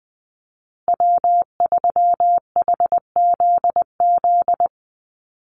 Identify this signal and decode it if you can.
Morse code, heard as a keyed tone, about 20 wpm, 700 Hz. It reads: W3HZZ
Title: Call Signs Repeated 3x - 20wpm